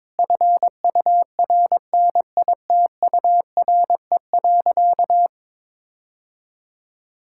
Morse code, heard as a keyed tone, about 22 wpm, 700 Hz. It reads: FURNITURE.
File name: practice / 22wpm / Sentences with 4-Word Phrases - 22wpm